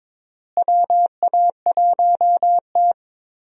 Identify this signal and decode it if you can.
Morse code, heard as a keyed tone, about 22 wpm, 700 Hz. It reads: WA1T
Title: USA Call Signs - 4 Characters - Repeated 3x - 22wpm